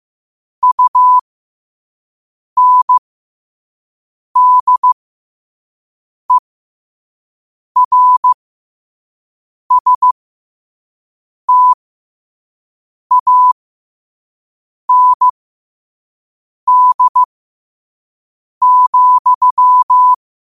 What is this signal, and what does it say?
Morse code, heard as a keyed tone, about 15 wpm, spaced as for 5 wpm, 1000 Hz. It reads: UNDERSTAND,
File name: qotdfiles / 5wpm